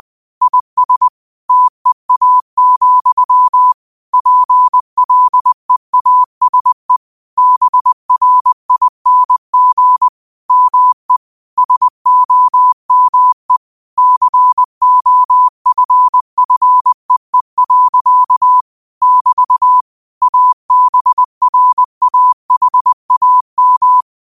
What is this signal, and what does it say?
Morse code, heard as a keyed tone, about 20 wpm, 1000 Hz. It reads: IS TEA, PLEASE BRING ME SOME COFFEE. <BT> ABRAHAM
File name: qotdfiles / 20wpm